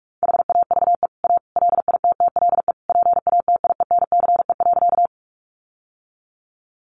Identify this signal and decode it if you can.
Morse code, heard as a keyed tone, about 44 wpm, 700 Hz. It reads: HAVE A LITTLE PATIENCE.
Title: Sentences from Top 1300 Words 45wpm